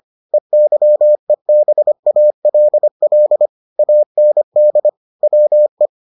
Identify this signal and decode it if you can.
Morse code, heard as a keyed tone, about 25 wpm, 600 Hz. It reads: EYEBALL AND WE